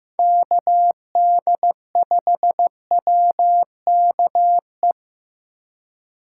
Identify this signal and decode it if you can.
Morse code, heard as a keyed tone, about 15 wpm, 700 Hz. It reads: KD5WKE
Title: USA Call Signs - 6 Characters - 3x Character Spacing - 15wpm